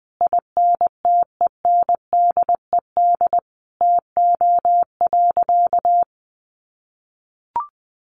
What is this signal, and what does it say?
Morse code, heard as a keyed tone, about 20 wpm, 700 Hz. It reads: INTENDED TO.
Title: A Sound Of Thunder 20wpm